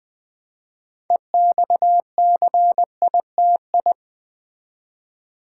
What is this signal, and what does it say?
Morse code, heard as a keyed tone, about 20 wpm, 700 Hz. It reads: EXCITI
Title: Build A Word - 2 Letters At A Time - 8-Letter Words - 20wpm